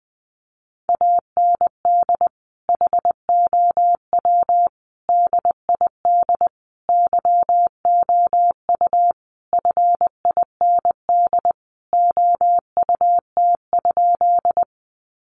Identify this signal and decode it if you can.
Morse code, heard as a keyed tone, about 19 wpm, 700 Hz. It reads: AND HOW DID YOU FIND OUT?
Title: Sentences from Top 100 Words 20wpm